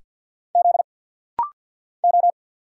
Morse code, heard as a keyed tone, about 49 wpm, 700 Hz. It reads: Z O
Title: Single Letter-Number Alt - Rapid-Fire 50wpm